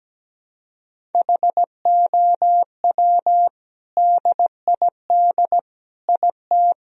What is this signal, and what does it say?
Morse code, heard as a keyed tone, about 17 wpm, 700 Hz. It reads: HOW DID IT
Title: Sentences from Top 1000 Words - Review Part 1 - 17wpm